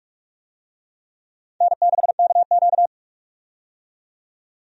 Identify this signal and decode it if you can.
Morse code, heard as a keyed tone, about 46 wpm, 700 Hz. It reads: N6KQ
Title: USA Call Signs - 4 Characters - 45wpm